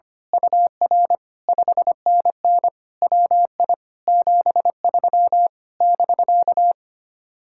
Morse code, heard as a keyed tone, about 24 wpm, 700 Hz. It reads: UR 5NN WI 73 <BK>
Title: Speed-Racing Common QSO Phrases Encore - 25wpm